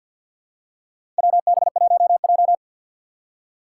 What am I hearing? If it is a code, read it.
Morse code, heard as a keyed tone, about 52 wpm, 700 Hz. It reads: WB1J
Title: USA Call Signs - 4 Characters - 3x Character Spacing - 50wpm